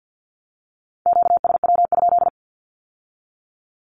Morse code, heard as a keyed tone, about 50 wpm, 700 Hz. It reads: QSW?
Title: Q-Codes with Spoken Meaning - 50wpm